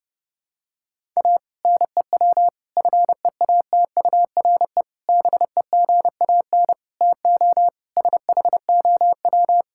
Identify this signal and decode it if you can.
Morse code, heard as a keyed tone, about 30 wpm, 700 Hz. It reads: A NEW FEATURE BEGAN TO SHOW
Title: Sentences from Top 1000 Words - Review Part 5 - 30wpm